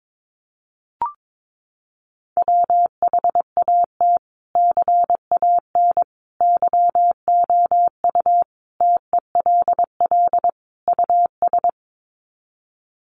Morse code, heard as a keyed tone, about 22 wpm, 700 Hz. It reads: WHAT CAN YOU TELL US
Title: Sets of 5 Words Encore 22wpm